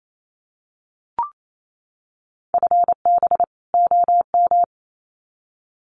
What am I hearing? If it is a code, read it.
Morse code, heard as a keyed tone, about 28 wpm, 700 Hz. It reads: FB OM
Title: Common QSO Phrases 28wpm